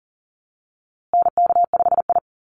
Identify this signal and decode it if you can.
Morse code, heard as a keyed tone, about 40 wpm, 700 Hz. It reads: NK5I
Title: USA Call Signs Repeated 2x 40wpm